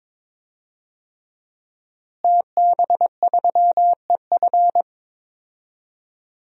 Morse code, heard as a keyed tone, about 22 wpm, 700 Hz. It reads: TB3EF
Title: DXCC-100 Call Signs - 5 Characters - Repeated 3x - 22wpm